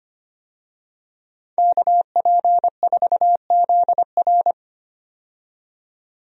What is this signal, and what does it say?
Morse code, heard as a keyed tone, about 25 wpm, 700 Hz. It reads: KP4ZR
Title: USA Call Signs - 5 Characters - 25wpm